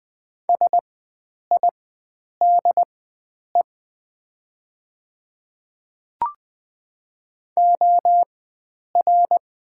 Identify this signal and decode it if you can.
Morse code, heard as a keyed tone, about 20 wpm, spaced as for 9 wpm, 700 Hz. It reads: SIDE OR
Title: 4x Character Spacing - Top 500 Words - 20wpm